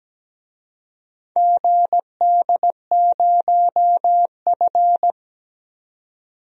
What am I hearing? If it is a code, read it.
Morse code, heard as a keyed tone, about 17 wpm, 700 Hz. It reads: GD0F
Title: DXCC-100 Call Signs - 4 Characters - 17wpm